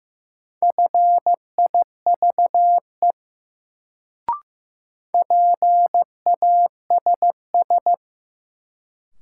Morse code, heard as a keyed tone, about 15 wpm, 700 Hz. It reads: FIVE PASS
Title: Sentences from Top 400 Words - New Word Drill - 15wpm